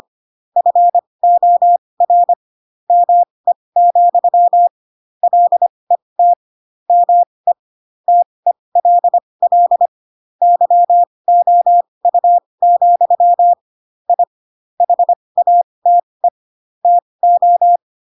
Morse code, heard as a keyed tone, about 25 wpm, spaced as for 20 wpm, 700 Hz. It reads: FOR ME, LET ME TELL YOU, I HATE TO
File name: Pinocchio / Pinocchio ch4 15 25 ik1hge